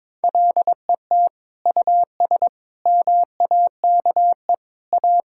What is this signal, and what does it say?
Morse code, heard as a keyed tone, about 22 wpm, 700 Hz. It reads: LET US MAKE A
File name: practice / 22wpm / Sentences with 2-Word Phrases - 22wpm